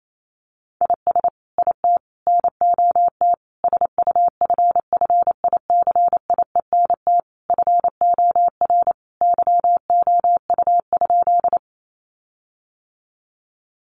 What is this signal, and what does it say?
Morse code, heard as a keyed tone, about 28 wpm, 700 Hz. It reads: IS IT NOT SUFFICIENT FOR YOU?
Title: Sentences from Top 1400 Words 28wpm